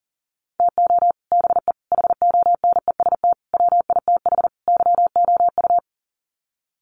Morse code, heard as a keyed tone, about 40 wpm, 700 Hz. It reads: TO BE HONEST WITH YOU
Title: Sets of 5 Words Encore 40wpm